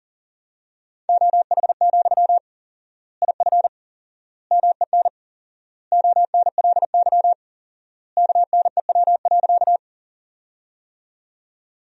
Morse code, heard as a keyed tone, about 40 wpm, 700 Hz. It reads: OH, IF MEN ONLY KNEW.
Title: 4x Word-Spacing - Sentences from Top 500 Words - 40wpm